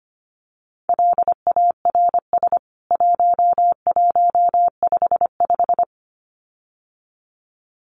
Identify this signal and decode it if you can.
Morse code, heard as a keyed tone, about 25 wpm, 700 Hz. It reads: LARS 1155
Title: CWT Exchange 25wpm